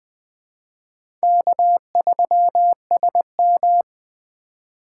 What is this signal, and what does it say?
Morse code, heard as a keyed tone, about 20 wpm, 700 Hz. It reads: K3SM
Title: USA Call Signs - 4 Characters - 4x Character Spacing - 20wpm